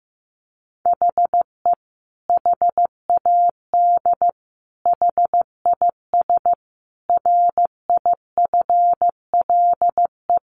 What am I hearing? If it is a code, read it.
Morse code, heard as a keyed tone, about 15 wpm, 700 Hz. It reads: HE HAD HIS RIFLE
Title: A Sound Of Thunder 15wpm